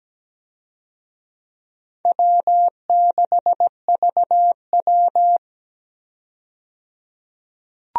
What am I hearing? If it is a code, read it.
Morse code, heard as a keyed tone, about 17 wpm, 700 Hz. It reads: W6VW
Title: USA Call Signs - 4 Characters - 17wpm